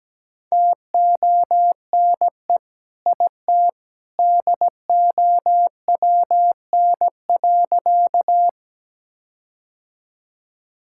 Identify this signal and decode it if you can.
Morse code, heard as a keyed tone, about 17 wpm, 700 Hz. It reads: TONE IT DOWN.
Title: Sentences from Top 1600 Words 17wpm